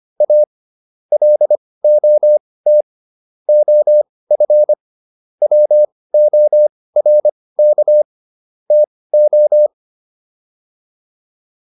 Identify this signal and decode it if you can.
Morse code, heard as a keyed tone, about 25 wpm, spaced as for 18 wpm, 600 Hz. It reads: A LOT OF WORK TO